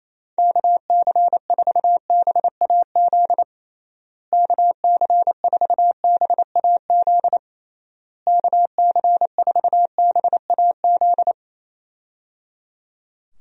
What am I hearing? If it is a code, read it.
Morse code, heard as a keyed tone, about 28 wpm, 700 Hz. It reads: KC4BAZ KC4BAZ KC4BAZ
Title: USA Call Signs - 6 Characters - Repeated 3x - 28wpm